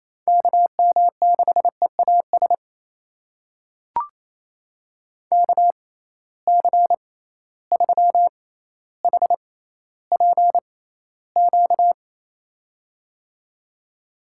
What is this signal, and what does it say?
Morse code, heard as a keyed tone, about 27 wpm, spaced as for 14 wpm, 700 Hz. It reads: KM6EAS KC3HPQ
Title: USA Call Signs - 6 Characters - 6x Character Spacing - 28wpm